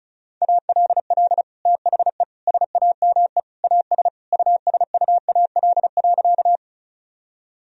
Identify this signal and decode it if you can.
Morse code, heard as a keyed tone, about 36 wpm, 700 Hz. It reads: ALL THE SAME AS USUAL.
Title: Sentences from Top 1000 Words - Review Part 2 - 35wpm